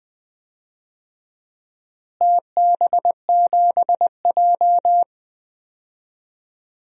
Morse code, heard as a keyed tone, about 20 wpm, 700 Hz. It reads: TB7J
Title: DXCC-100 Call Signs - 4 Characters - Repeated 3x - 20wpm